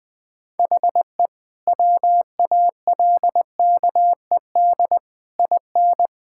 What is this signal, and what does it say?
Morse code, heard as a keyed tone, about 20 wpm, 700 Hz. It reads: HE WALKED IN
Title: Sentences with 5-Word Phrases - 20wpm